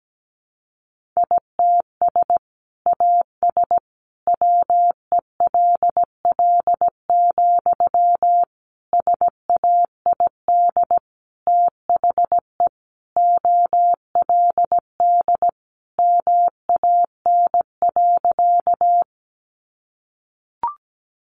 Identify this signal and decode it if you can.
Morse code, heard as a keyed tone, about 17 wpm, 700 Hz. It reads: ITS AS WELL, SAID THE OLD MAN.
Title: Sentences from Top 600 Words 17wpm